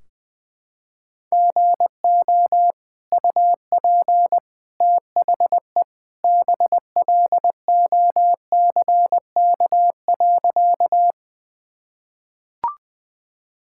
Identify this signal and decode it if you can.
Morse code, heard as a keyed tone, about 21 wpm, 700 Hz. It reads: GO UP THE BLOCK.
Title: Sentences from Top 1600 Words 20wpm